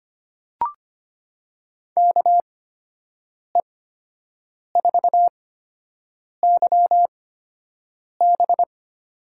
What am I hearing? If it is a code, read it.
Morse code, heard as a keyed tone, about 25 wpm, spaced as for 7 wpm, 700 Hz. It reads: KE4YB
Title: USA Call Signs - 8x Character Spacing - 25wpm